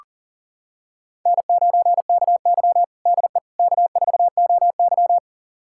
Morse code, heard as a keyed tone, about 37 wpm, 700 Hz. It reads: N9KY DE K4OY
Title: USA Callsign DE Callsign - 4 Characters - 40wpm